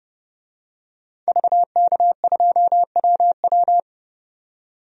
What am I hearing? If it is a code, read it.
Morse code, heard as a keyed tone, about 30 wpm, 700 Hz. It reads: VK2WW
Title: DXCC-100 Call Signs - 5 Characters - 30wpm